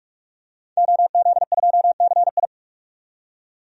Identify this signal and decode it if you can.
Morse code, heard as a keyed tone, about 46 wpm, 700 Hz. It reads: OZ2CI
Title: DXCC-100 Call Signs - 5 Characters - 45wpm